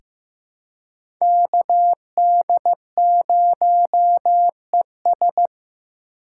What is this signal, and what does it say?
Morse code, heard as a keyed tone, about 15 wpm, 700 Hz. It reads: KD0ES
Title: USA Call Signs - 5 Characters - 4x Character Spacing - 15wpm